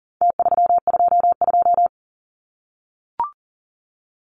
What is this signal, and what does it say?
Morse code, heard as a keyed tone, about 40 wpm, 700 Hz. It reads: T322
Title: Numbers - Contest Style - 4-Digit 40wpm